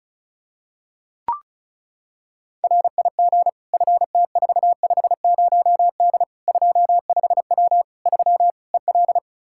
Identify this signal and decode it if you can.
Morse code, heard as a keyed tone, about 36 wpm, 700 Hz. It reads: RIG FT450D 25W 3 EL
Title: Practice QSO - Exchange 2 - Info - 35wpm